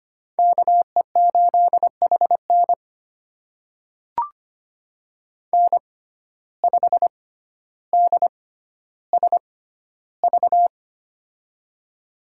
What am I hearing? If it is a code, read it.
Morse code, heard as a keyed tone, about 25 wpm, spaced as for 13 wpm, 700 Hz. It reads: KE8HN N5DSV
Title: USA Call Signs - 5 Characters - 6x Character Spacing - 25wpm